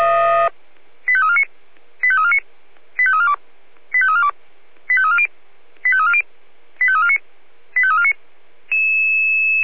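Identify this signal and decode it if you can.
Five-tone selective call (ZVEI), heard as quick runs of five tones, tones of about 70 ms. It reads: 85349 85349 85342 85342 85340 85340 85349 85349
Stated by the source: Received on SDR, ZVEI five-tone call 85330.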